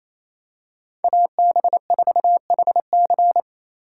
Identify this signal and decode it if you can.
Morse code, heard as a keyed tone, about 28 wpm, 700 Hz. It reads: AB4HC
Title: Speed Racing - USA 2x2 Call Signs 28wpm